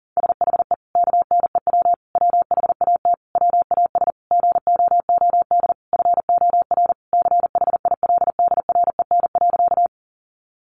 Morse code, heard as a keyed tone, about 40 wpm, 700 Hz. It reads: SHE KNEW WHAT WAS GOOD FOR CHILDREN.
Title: Sentences from Top 400 Words 40wpm